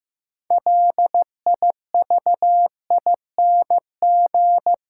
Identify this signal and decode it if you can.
Morse code, heard as a keyed tone, about 15 wpm, 700 Hz. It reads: LIVING
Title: Sets of 4 Words - New Word Drill - 15wpm